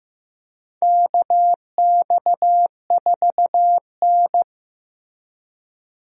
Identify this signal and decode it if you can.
Morse code, heard as a keyed tone, about 15 wpm, 700 Hz. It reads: KX4N
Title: USA Call Signs - 4 Characters - Repeated 2x - 15wpm